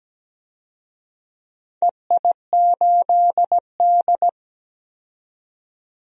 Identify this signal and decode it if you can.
Morse code, heard as a keyed tone, about 17 wpm, 700 Hz. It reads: EI8D
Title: DXCC-100 Call Signs - 4 Characters - 17wpm